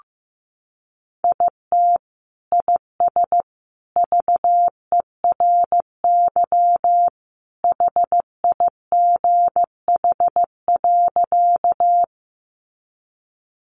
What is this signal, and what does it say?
Morse code, heard as a keyed tone, about 15 wpm, 700 Hz. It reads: IT IS VERY HIGH.
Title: Sentences from Top 200 Words 15wpm